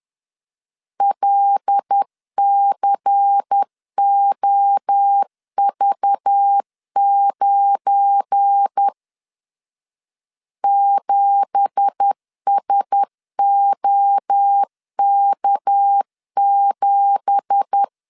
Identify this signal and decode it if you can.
Morse code, heard as a keyed tone, about 10 wpm, 800 Hz. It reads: LCOV9 7SOK7
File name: LEZIONI / 9 2867 10wpm 2733